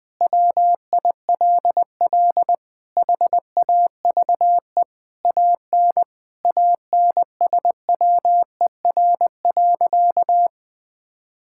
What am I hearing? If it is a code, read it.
Morse code, heard as a keyed tone, about 20 wpm, 700 Hz. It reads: WILL HAVE AN ANSWER.
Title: Sentences with 5-Word Phrases - 20wpm